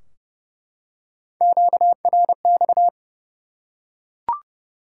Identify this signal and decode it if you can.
Morse code, heard as a keyed tone, about 30 wpm, 700 Hz. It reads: QRX
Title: Top 180 QSO Elements 30wpm